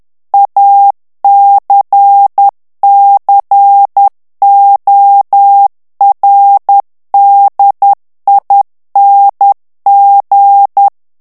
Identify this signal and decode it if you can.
Morse code, heard as a keyed tone, about 10 wpm, 800 Hz. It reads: ACCORDING
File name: ANGLAIS texte clair 2 vitesse 10 mots minute